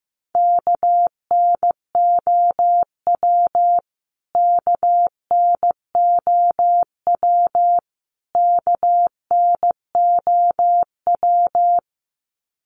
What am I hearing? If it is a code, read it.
Morse code, heard as a keyed tone, about 15 wpm, 700 Hz. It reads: KNOW KNOW KNOW
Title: Top 500 Words Repeated 15wpm